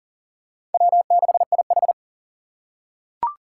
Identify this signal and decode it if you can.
Morse code, heard as a keyed tone, about 40 wpm, 690 Hz. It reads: W6IH
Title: USA Call Signs - 4 Characters - 2x Character Spacing - 40wpm